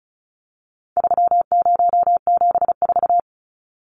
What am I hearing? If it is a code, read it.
Morse code, heard as a keyed tone, about 34 wpm, 700 Hz. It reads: 3074
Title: Numbers - 4-Digit 35wpm